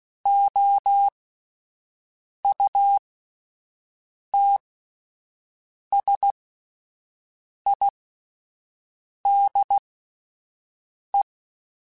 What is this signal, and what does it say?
Morse code, heard as a keyed tone, about 16 wpm, spaced as for 5 wpm, 800 Hz. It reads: OUTSIDE
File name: Spaced Characters / Spaced Characters-mp3s / Weather 5 1 sp @10